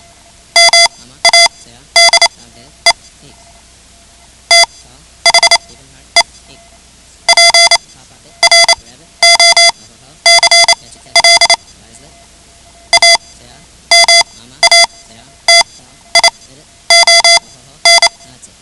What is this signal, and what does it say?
Morse code, heard as a keyed tone, about 28 wpm, spaced as for 12 wpm, 2050 Hz. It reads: MADE THE PROCLAMATION
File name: ham radio / CodeQuick / Code Quick 10